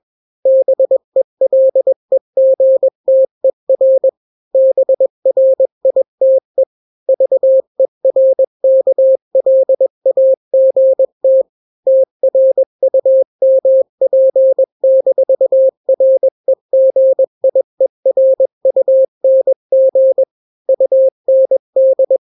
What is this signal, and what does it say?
Morse code, heard as a keyed tone, about 21 wpm, 530 Hz. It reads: BELEGTER BRITE VERKLAGT TRUMP-REGIERUNG UND
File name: daily / 20wpm